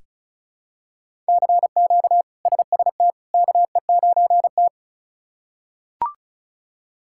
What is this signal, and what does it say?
Morse code, heard as a keyed tone, about 34 wpm, 700 Hz. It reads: CQ SST KE9T
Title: K1USN SST QSO Practice - 35wpm